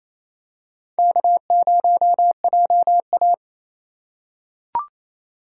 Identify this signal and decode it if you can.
Morse code, heard as a keyed tone, about 28 wpm, 700 Hz. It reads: K0JA
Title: USA Call Signs - 4 Characters - 2x Character Spacing - 28wpm